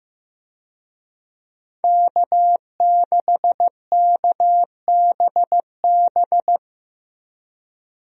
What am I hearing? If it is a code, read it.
Morse code, heard as a keyed tone, about 16 wpm, 700 Hz. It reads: K6KBB
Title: USA Call Signs - 5 Characters - Repeated 3x - 15wpm